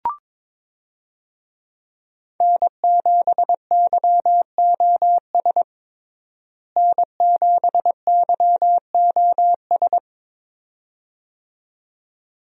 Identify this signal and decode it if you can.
Morse code, heard as a keyed tone, about 22 wpm, 700 Hz. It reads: N7YOS N7YOS
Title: USA Call Signs - 5 Characters - Repeated 2x - 22wpm